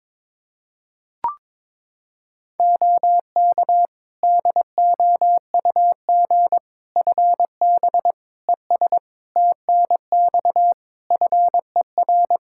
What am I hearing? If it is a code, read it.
Morse code, heard as a keyed tone, about 22 wpm, 700 Hz. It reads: OK DOUG FB ES TNX FER
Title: K1USN SST QSO Practice 22wpm